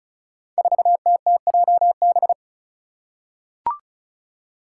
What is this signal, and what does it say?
Morse code, heard as a keyed tone, about 34 wpm, 700 Hz. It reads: 4TTJB
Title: Five Challenging Letter-Number Combinations 35wpm